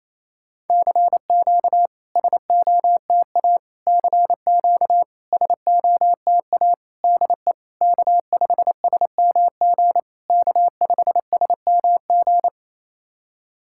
Morse code, heard as a keyed tone, about 27 wpm, 660 Hz. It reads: CQ SOTA CQ SOTA DE K5SMG K5SMG
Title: SOTA QSO Practice 28wpm